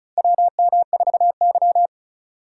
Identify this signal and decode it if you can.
Morse code, heard as a keyed tone, about 36 wpm, 700 Hz. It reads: WM4Y
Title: USA Call Signs - 4 Characters - 3x Character Spacing - 35wpm